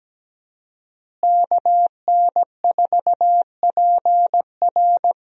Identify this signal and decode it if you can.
Morse code, heard as a keyed tone, about 17 wpm, 700 Hz. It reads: KN4PR
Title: USA Call Signs - 5 Characters - Repeated 2x - 17wpm